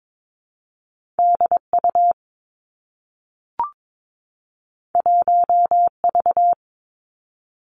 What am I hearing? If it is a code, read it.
Morse code, heard as a keyed tone, about 22 wpm, 700 Hz. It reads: D U 1 V
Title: Two Challenging Letter-Number Combinations 22wpm